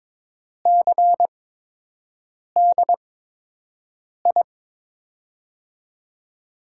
Morse code, heard as a keyed tone, about 21 wpm, spaced as for 6 wpm, 700 Hz. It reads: CDI
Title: USA Call Signs - 6 Characters - 8x Character Spacing - 22wpm